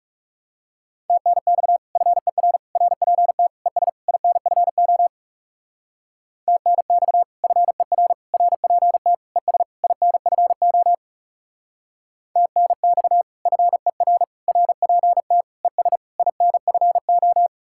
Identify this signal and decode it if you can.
Morse code, heard as a keyed tone, about 36 wpm, 700 Hz. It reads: TNX FER RPT ES INFO TNX FER RPT ES INFO TNX FER RPT ES INFO
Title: Speed-Racing Common QSO Phrases - 30wpm